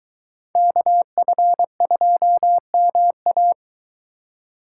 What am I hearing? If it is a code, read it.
Morse code, heard as a keyed tone, about 23 wpm, 700 Hz. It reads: KF2MA